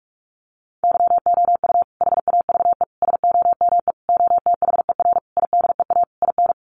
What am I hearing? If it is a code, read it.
Morse code, heard as a keyed tone, about 44 wpm, 700 Hz. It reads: YOU HAVE SOME OTHER IDEA IN
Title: Sentences from Top 1300 Words 45wpm